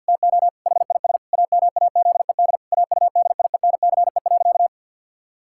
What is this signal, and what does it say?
Morse code, heard as a keyed tone, about 51 wpm, 700 Hz. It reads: TO HIS AMAZED AUDIENCE.
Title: Pangram Challenge v2 - 50wpm